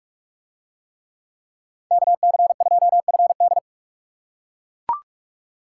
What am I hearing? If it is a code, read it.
Morse code, heard as a keyed tone, about 44 wpm, 700 Hz. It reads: KC2FD